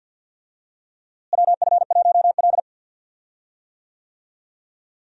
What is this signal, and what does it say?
Morse code, heard as a keyed tone, about 52 wpm, 700 Hz. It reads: WF1L
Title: USA Call Signs - 4 Characters - Repeated 3x - 50wpm